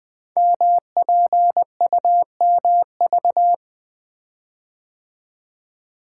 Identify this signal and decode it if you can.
Morse code, heard as a keyed tone, about 20 wpm, 700 Hz. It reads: MPUMV
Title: Code Groups - 5 Letters - 20wpm